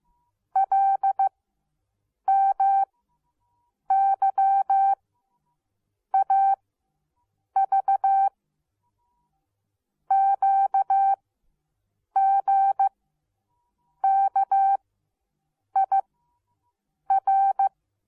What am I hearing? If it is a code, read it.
Morse code, heard as a keyed tone, about 15 wpm, spaced as for 6 wpm, 780 Hz. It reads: LMYAV QGKIR